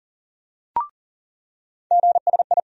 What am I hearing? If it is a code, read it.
Morse code, heard as a keyed tone, about 40 wpm, 700 Hz. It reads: GSI
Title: USA Call Sign Postfixes - 3 Letters 40wpm